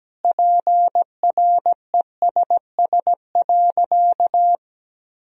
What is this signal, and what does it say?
Morse code, heard as a keyed tone, about 17 wpm, 700 Hz. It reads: PRESS.
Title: Sentences from Top 1000 Words - Review Part 1 - 17wpm